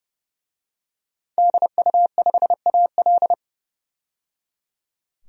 Morse code, heard as a keyed tone, about 30 wpm, 700 Hz. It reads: DU5AL
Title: DXCC-100 Call Signs - 5 Characters - 30wpm